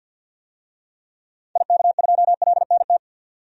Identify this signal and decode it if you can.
Morse code, heard as a keyed tone, about 52 wpm, 700 Hz. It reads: IK2LNT
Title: DXCC-100 Call Signs - 6 Characters - 50wpm